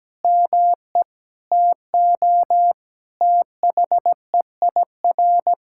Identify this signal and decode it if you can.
Morse code, heard as a keyed tone, about 17 wpm, 700 Hz. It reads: ME TO THEIR
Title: Sentences from Top 1600 Words 17wpm